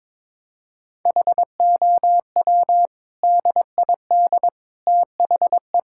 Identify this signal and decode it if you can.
Morse code, heard as a keyed tone, about 22 wpm, 700 Hz. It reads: HOW DID THE
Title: Sentences from Top 1000 Words - Review Part 5 - 22wpm